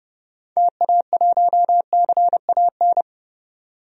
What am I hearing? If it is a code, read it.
Morse code, heard as a keyed tone, about 30 wpm, 700 Hz. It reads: TA1CAN
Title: DXCC-100 Call Signs - 6 Characters - 30wpm